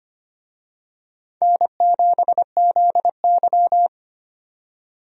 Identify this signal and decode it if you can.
Morse code, heard as a keyed tone, about 25 wpm, 700 Hz. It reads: N7ZY
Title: USA Call Signs - 4 Characters - 25wpm